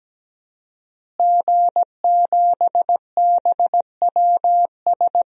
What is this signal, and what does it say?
Morse code, heard as a keyed tone, about 17 wpm, 700 Hz. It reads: G7BWS
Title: DXCC-100 Call Signs - 5 Characters - Repeated 3x - 17wpm